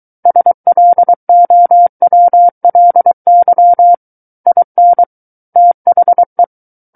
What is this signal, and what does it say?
Morse code, heard as a keyed tone, about 23 wpm, 700 Hz. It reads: SLOWLY IN THE
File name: Book One / 14-Chapter 14 - 23 WPM